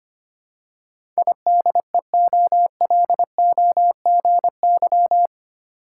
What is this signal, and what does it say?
Morse code, heard as a keyed tone, about 25 wpm, 690 Hz. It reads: IDEOLOGY
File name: 8 Letter Words - First 3 Letters Given - 25wpm